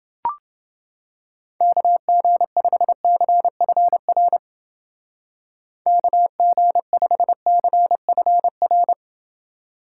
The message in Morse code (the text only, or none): KG5CFR KG5CFR